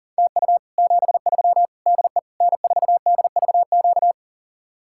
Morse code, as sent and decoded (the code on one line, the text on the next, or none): - ..- / --... ...-- / -.. / -. ....- -.. ...- --.-
TU 73 DE N4DVQ